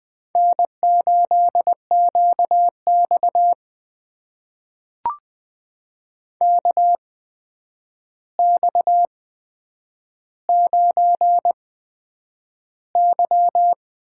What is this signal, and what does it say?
Morse code, heard as a keyed tone, about 20 wpm, spaced as for 9 wpm, 700 Hz. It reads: N8QX KX9Y
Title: USA Call Signs - 4 Characters - 8x Character Spacing - 20wpm